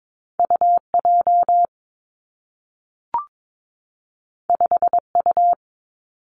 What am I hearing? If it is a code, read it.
Morse code, heard as a keyed tone, about 22 wpm, 700 Hz. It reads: U J 5 U
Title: Two Challenging Letter-Number Combinations 22wpm